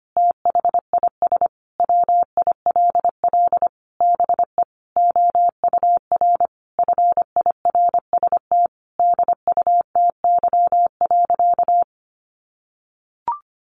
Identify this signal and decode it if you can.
Morse code, heard as a keyed tone, about 25 wpm, 700 Hz. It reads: THIS WILL BE OUR FIRST DUTY.
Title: Sentences from Top 1500 Words 25wpm